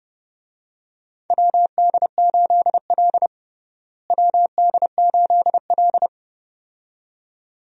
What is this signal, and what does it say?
Morse code, heard as a keyed tone, about 29 wpm, 700 Hz. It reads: WD8L WD8L
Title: USA Call Signs - 4 Characters - Repeated 2x - 30wpm